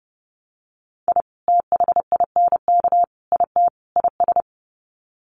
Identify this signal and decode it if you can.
Morse code, heard as a keyed tone, about 30 wpm, 700 Hz. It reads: I THINK IT IS
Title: Sets of 4 Words 30wpm